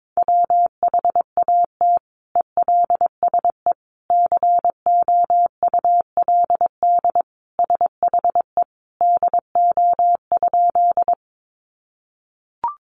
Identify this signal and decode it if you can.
Morse code, heard as a keyed tone, about 22 wpm, 700 Hz. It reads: WHAT ELSE COULD SHE DO?
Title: Sentences from Top 800 Words 22wpm